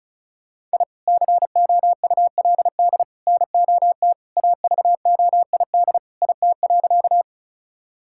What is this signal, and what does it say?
Morse code, heard as a keyed tone, about 36 wpm, 690 Hz. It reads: I COULD NOT AVOID IT.
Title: Sentences from Top 1000 Words - Review Part 6 - 35wpm